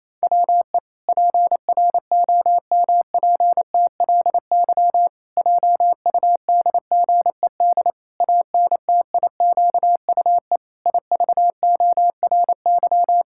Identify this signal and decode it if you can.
Morse code, heard as a keyed tone, about 27 wpm, 700 Hz. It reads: WE PROMPTLY JUDGED ANTIQUE IVORY
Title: Pangram Challenge v2 - 28wpm